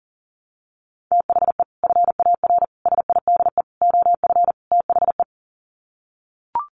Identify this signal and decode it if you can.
Morse code, heard as a keyed tone, about 40 wpm, 700 Hz. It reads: THE FAR SIDE OF THE
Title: Sets of 5 Words 40wpm